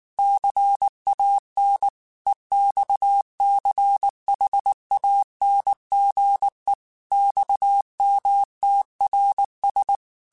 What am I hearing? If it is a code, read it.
Morse code, heard as a keyed tone, about 19 wpm, 800 Hz. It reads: CAN EXCHANGE XMTRS
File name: QSOs / QSOs-mp3s / WA7QOX 1@18